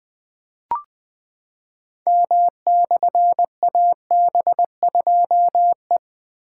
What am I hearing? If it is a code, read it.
Morse code, heard as a keyed tone, about 20 wpm, 700 Hz. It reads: M/AB2E
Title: Difficult Call Signs 20wpm